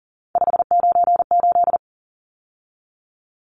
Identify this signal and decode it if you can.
Morse code, heard as a keyed tone, about 40 wpm, 700 Hz. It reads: 598
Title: Numbers - 3-Digit 40wpm